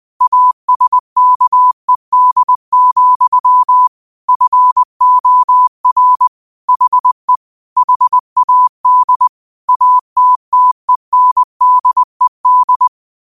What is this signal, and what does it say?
Morse code, heard as a keyed tone, about 20 wpm, 1000 Hz. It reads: ASKED, FOR HE HAD ATTENDED